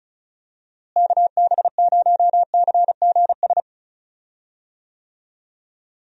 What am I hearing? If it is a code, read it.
Morse code, heard as a keyed tone, about 36 wpm, 700 Hz. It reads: KB0CGS
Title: USA Call Signs - 6 Characters - Repeated 2x - 35wpm